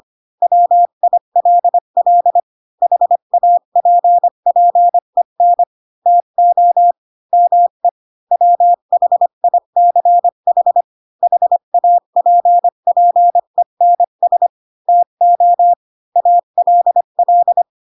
Morse code, heard as a keyed tone, about 25 wpm, 700 Hz. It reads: WILL HAPPEN TO ME WHICH HAPPENS TO ALL
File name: Pinocchio / Pinocchio ch4 20 25 ik1hge